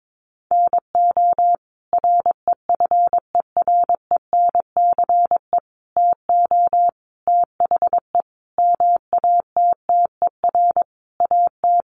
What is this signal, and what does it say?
Morse code, heard as a keyed tone, about 22 wpm, 700 Hz. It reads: NO REFERENCE TO THE MATTER AT
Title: Sentences from Top 1200 Words 22wpm